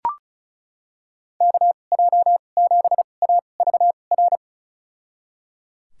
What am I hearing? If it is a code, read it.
Morse code, heard as a keyed tone, about 36 wpm, spaced as for 25 wpm, 700 Hz. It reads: KJ7AVR
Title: USA Call Signs - 2x Character Spacing - 35wpm